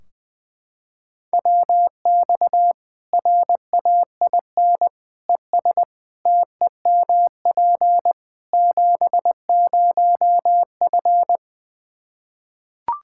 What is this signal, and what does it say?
Morse code, heard as a keyed tone, about 20 wpm, 700 Hz. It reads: WX RAIN ES TEMP 70F
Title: Practice QSO - Exchange 2 - Info - 20wpm